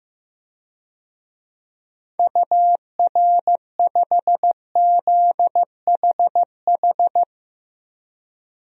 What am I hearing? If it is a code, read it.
Morse code, heard as a keyed tone, about 16 wpm, 700 Hz. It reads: UR5ZHH
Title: DXCC-100 Call Signs - 6 Characters - Repeated 3x - 15wpm